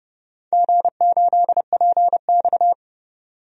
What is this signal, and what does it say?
Morse code, heard as a keyed tone, about 31 wpm, 700 Hz. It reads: G8PX
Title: DXCC-100 Call Signs - 4 Characters - 30wpm